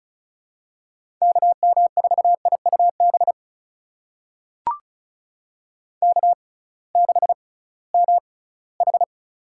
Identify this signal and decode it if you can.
Morse code, heard as a keyed tone, about 34 wpm, spaced as for 20 wpm, 700 Hz. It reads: KM4IUB K6MH
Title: USA Call Signs - 6x Character Spacing - 35wpm